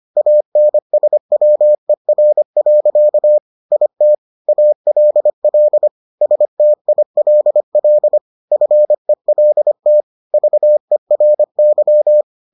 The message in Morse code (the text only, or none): ANSWER. IT ALL STILL FELT VERY